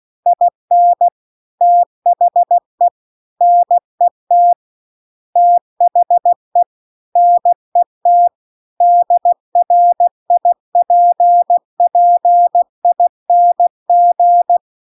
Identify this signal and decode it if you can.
Morse code, heard as a keyed tone, about 16 wpm, 700 Hz. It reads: IN THE NET THE NET DRIPPING